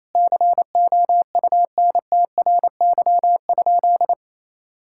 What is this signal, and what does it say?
Morse code, heard as a keyed tone, about 28 wpm, 700 Hz. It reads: COUNTRY?
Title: Sentences from Top 1000 Words - Review Part 5 - 28wpm